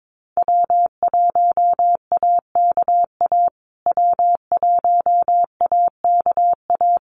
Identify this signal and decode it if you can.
Morse code, heard as a keyed tone, about 22 wpm, 700 Hz. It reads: W1AKA W1AKA
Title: USA Call Signs Repeated 3x 22wpm